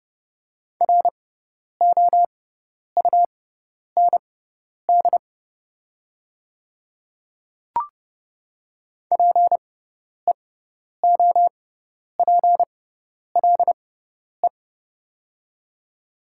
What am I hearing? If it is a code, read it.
Morse code, heard as a keyed tone, about 30 wpm, spaced as for 10 wpm, 700 Hz. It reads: ROUND PEOPLE
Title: 6x Character Spacing - Top 200 Words - 30wpm